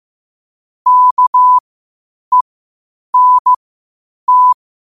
Morse code, heard as a keyed tone, about 15 wpm, spaced as for 8 wpm, 1000 Hz. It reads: KENT
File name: qotdfiles / 8wpm